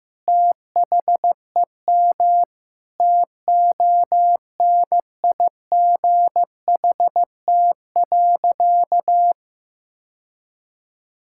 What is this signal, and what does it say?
Morse code, heard as a keyed tone, about 15 wpm, 700 Hz. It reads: THEM TONIGHT.
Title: Sentences from Top 1000 Words - Review Part 6 - 15wpm